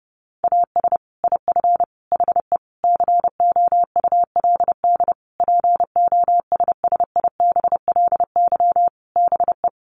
Morse code, heard as a keyed tone, about 28 wpm, 700 Hz. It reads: AS IF HE COULD POSSIBLY BE
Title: Sentences from Top 1300 Words 30wpm